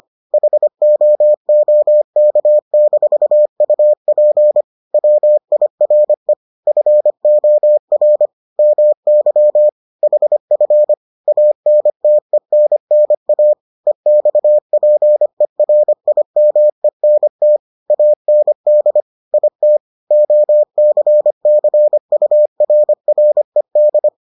Morse code, heard as a keyed tone, about 25 wpm, 600 Hz. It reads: HOOK-UP WIRE FOR MY HF ANTENNA EXPERIMENT AND IT OCCURRED